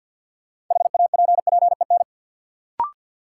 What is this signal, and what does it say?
Morse code, heard as a keyed tone, about 47 wpm, 700 Hz. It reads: HAPPEN
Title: Sets of 5 Words Encore - New Word Drill - 50wpm